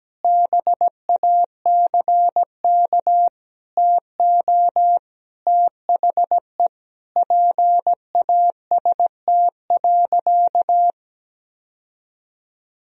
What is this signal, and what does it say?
Morse code, heard as a keyed tone, about 17 wpm, 700 Hz. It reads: BACK TO THE PAST.
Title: Sentences from Top 1000 Words - Review Part 3 - 17wpm